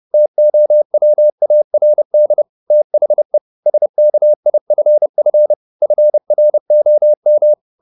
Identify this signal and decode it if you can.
Morse code, heard as a keyed tone, about 30 wpm, 600 Hz. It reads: TOWARD THE SKIFF FROM
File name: the old man and the sea / the old man and the sea 1 30wpm